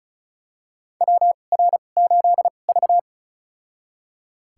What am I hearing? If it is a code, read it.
Morse code, heard as a keyed tone, about 34 wpm, spaced as for 25 wpm, 700 Hz. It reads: WR8V
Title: USA Call Signs - 4 Characters - 2x Character Spacing - 35wpm